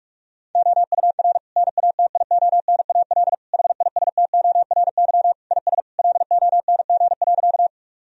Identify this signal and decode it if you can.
Morse code, heard as a keyed tone, about 46 wpm, 700 Hz. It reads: OUR NATIONAL HISTORY IS LONG.
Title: Sentences from Top 1000 Words - Review Part 4 - 45wpm